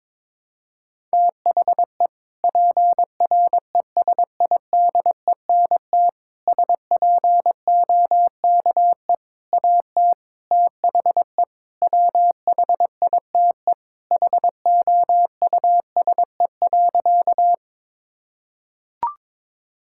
Morse code, heard as a keyed tone, about 22 wpm, 700 Hz. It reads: THE PRESIDENT SPOKE AT THE WHITE HOUSE.
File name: practice / 22wpm / Sentences with 4-Word Phrases - 22wpm